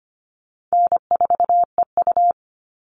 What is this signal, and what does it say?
Morse code, heard as a keyed tone, about 25 wpm, 700 Hz. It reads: N4EU
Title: All Cut Numbers - 4-Digit 25wpm